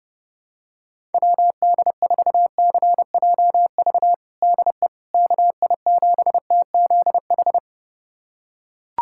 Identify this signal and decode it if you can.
Morse code, heard as a keyed tone, about 30 wpm, 700 Hz. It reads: WD4CJV DE KI7TZH
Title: USA Callsign DE Callsign - 6 Characters - 30wpm